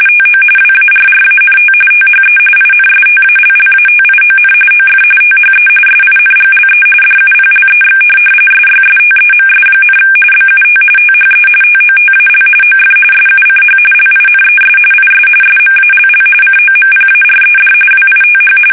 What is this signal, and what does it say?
RTTY, about 45.45 baud. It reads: TO WARRANT THE INSTALLATION OF THE SINGLE DUPLEX SYSTEMS, BUT THE TOTAL OF SUCH LINES IS SMALL COMPARED WITH
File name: fsk850